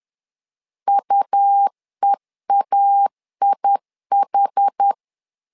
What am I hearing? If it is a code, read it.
Morse code, heard as a keyed tone, about 10 wpm, 800 Hz. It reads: UEAIH
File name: LEZIONI / 2 UHDN / 2 UH 10wpm 1714